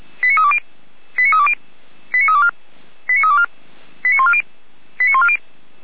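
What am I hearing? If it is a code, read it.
Five-tone selective call (ZVEI), heard as quick runs of five tones, tones of about 70 ms. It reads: 89320 89320 89324 89324 89140 89140
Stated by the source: ZVEI five-tone selective call received on SDR, from audio.f85360.feuerwehr-orlamuende.org fire brigade server.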